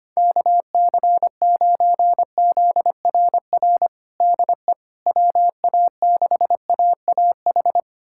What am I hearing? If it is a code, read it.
Morse code, heard as a keyed tone, about 25 wpm, 700 Hz. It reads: KC9ZRR DE WA6AAH
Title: USA Callsign DE Callsign 25wpm